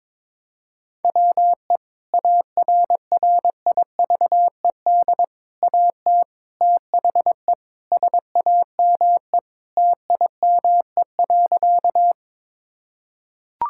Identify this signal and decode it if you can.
Morse code, heard as a keyed tone, about 22 wpm, 700 Hz. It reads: WE ARRIVED AT THE SAME TIME.
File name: practice / 22wpm / Sentences with 4-Word Phrases - 22wpm